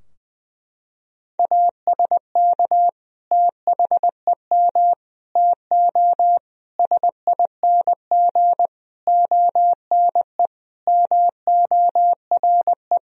Morse code, heard as a keyed tone, about 20 wpm, 700 Hz. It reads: ASK THEM TO SING ONE MORE
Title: Sentences from Top 1000 Words - Review Part 2 - 20wpm